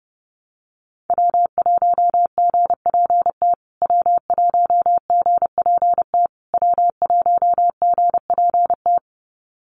Morse code, heard as a keyed tone, about 30 wpm, 700 Hz. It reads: W1GPT W1GPT W1GPT
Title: USA Call Signs Repeated 3x 30wpm